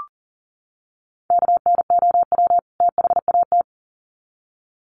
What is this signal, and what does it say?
Morse code, heard as a keyed tone, about 40 wpm, 700 Hz. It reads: KNOW THAT
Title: Sets of 2 Words 40wpm